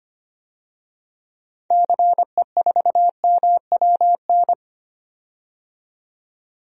Morse code, heard as a keyed tone, about 25 wpm, 700 Hz. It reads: CE4MWN
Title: DXCC-100 Call Signs - 6 Characters - 25wpm